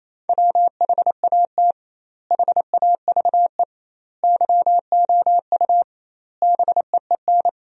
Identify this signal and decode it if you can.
Morse code, heard as a keyed tone, about 28 wpm, 700 Hz. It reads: WHAT HAVE YOU BEEN
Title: 2x Word-Spacing - Sentences from Top 200 Words - 28wpm